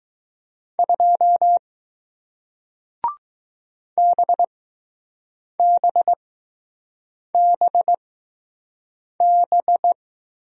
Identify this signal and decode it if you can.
Morse code, heard as a keyed tone, about 19 wpm, spaced as for 6 wpm, 700 Hz. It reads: 2 BBBB